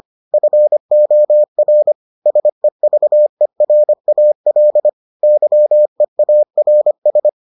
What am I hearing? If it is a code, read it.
Morse code, heard as a keyed tone, about 25 wpm, 600 Hz. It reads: FOR SEVERAL YEARS